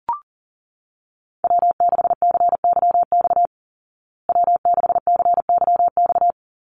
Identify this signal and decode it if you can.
Morse code, heard as a keyed tone, about 40 wpm, 700 Hz. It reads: W6CYX W6CYX
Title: Call Signs Repeated 3x - 40wpm